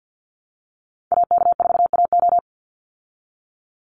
Morse code, heard as a keyed tone, about 48 wpm, 700 Hz. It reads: AK4AO
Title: Call Signs Repeated 3x - 50wpm